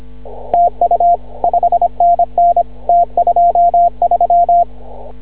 Morse code, heard as a keyed tone, about 26 wpm, 700 Hz. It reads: TU 5NN T23
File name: CW / ha6oi 2